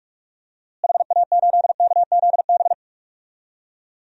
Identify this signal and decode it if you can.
Morse code, heard as a keyed tone, about 46 wpm, 700 Hz. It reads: HA8KZB
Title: DXCC-100 Call Signs - 6 Characters - Repeated 3x - 45wpm